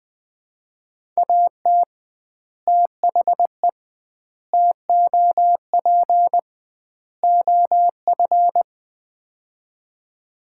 Morse code, heard as a keyed tone, about 20 wpm, 700 Hz. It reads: AT THE TOP OF
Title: 2x Word-Spacing - Sets of 4 Words - 20wpm